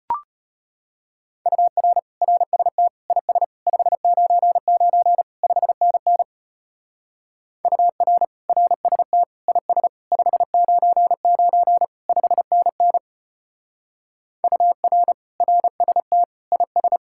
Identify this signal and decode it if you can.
Morse code, heard as a keyed tone, about 32 wpm, 700 Hz. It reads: UR RST IS 599 5NN UR RST IS 599 5NN UR RST IS